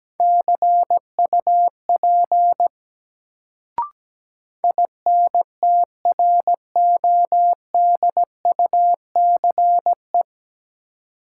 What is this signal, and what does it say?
Morse code, heard as a keyed tone, about 17 wpm, 700 Hz. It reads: CUP INTRODUCE
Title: Sentences from Top 1600 Words - New Word Drill Alt - 17wpm